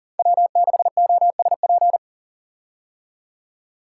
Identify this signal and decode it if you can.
Morse code, heard as a keyed tone, about 40 wpm, 700 Hz. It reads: W6OSP
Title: K1USN SST QSO Practice - 40wpm